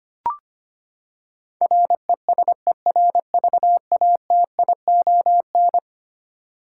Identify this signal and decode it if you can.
Morse code, heard as a keyed tone, about 25 wpm, 700 Hz. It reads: RESERVATION
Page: morse-code-ninja-practice-25wpm